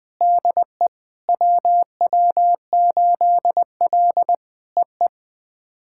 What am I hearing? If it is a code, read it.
Morse code, heard as a keyed tone, about 20 wpm, 700 Hz. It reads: DE WW8L EE
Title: POTA QSO Practice - 20wpm